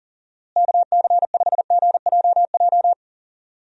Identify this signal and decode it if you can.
Morse code, heard as a keyed tone, about 40 wpm, 700 Hz. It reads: KC5GJJ
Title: USA Call Signs - 6 Characters - Repeated 2x - 40wpm